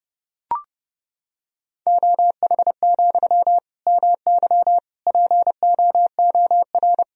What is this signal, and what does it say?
Morse code, heard as a keyed tone, about 31 wpm, 700 Hz. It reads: OH, MY POOR
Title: Sentences from Top 1000 Words - Review Part 3 - 30wpm